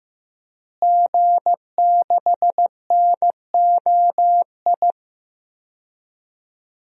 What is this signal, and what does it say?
Morse code, heard as a keyed tone, about 15 wpm, 700 Hz. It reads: G6NOI
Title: DXCC-100 Call Signs - 5 Characters - Repeated 3x - 15wpm